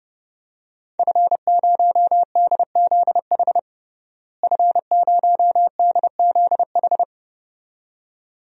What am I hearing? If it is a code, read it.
Morse code, heard as a keyed tone, about 30 wpm, 700 Hz. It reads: F0DZH F0DZH
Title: DXCC-100 Call Signs - 5 Characters - Repeated 2x - 30wpm